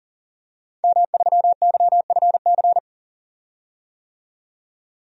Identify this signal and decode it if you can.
Morse code, heard as a keyed tone, about 40 wpm, 700 Hz. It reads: M3YFC
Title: DXCC-100 Call Signs - 5 Characters - 40wpm